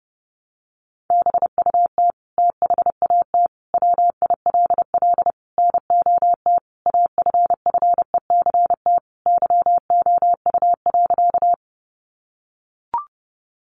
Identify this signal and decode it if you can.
Morse code, heard as a keyed tone, about 30 wpm, 700 Hz. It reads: BUT THAT WILL NOT AFFECT YOU.
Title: Sentences from Top 900 Words 30wpm